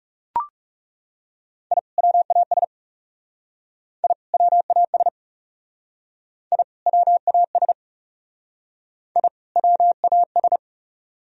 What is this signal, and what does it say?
Morse code, heard as a keyed tone, about 38 wpm, 700 Hz. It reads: I WAS I WAS I WAS I WAS